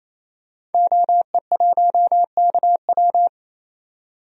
Morse code, heard as a keyed tone, about 28 wpm, 700 Hz. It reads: OE1KW
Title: DXCC-100 Call Signs - 5 Characters - 28wpm